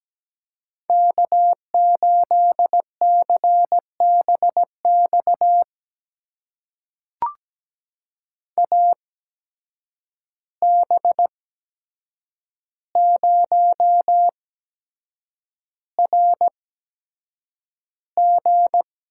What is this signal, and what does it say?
Morse code, heard as a keyed tone, about 17 wpm, spaced as for 7 wpm, 700 Hz. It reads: K8CBX AB0RG
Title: USA Call Signs - 5 Characters - 8x Character Spacing - 17wpm